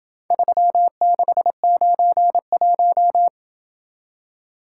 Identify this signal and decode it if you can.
Morse code, heard as a keyed tone, about 26 wpm, 700 Hz. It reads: 3691